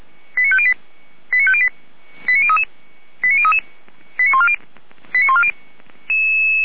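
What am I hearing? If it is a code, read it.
Five-tone selective call (ZVEI), heard as quick runs of five tones, tones of about 70 ms. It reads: 89508 89508 89033 89033 89140 89140
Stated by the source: ZVEI five-tone call 89058 received on SDR.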